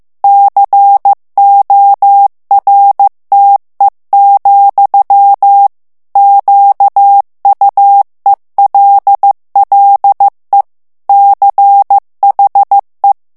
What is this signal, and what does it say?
Morse code, heard as a keyed tone, about 15 wpm, 800 Hz. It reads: CORTE, QUELLE CHE
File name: ITALIEN texte clair 1 vitesse 14 mots minute